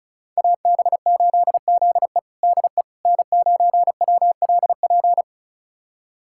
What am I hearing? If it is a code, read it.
Morse code, heard as a keyed tone, about 34 wpm, 700 Hz. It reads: AB8ZE DE N9WLP
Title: USA Callsign DE Callsign - 5 Characters - 35wpm